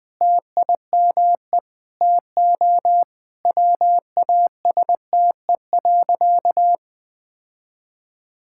Morse code, heard as a keyed tone, about 21 wpm, 700 Hz. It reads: TIME TO WASTE.
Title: Sentences with 3-Word Phrases - 20wpm